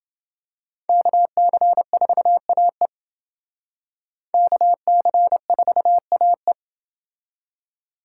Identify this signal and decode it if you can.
Morse code, heard as a keyed tone, about 27 wpm, 700 Hz. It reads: KC4AE KC4AE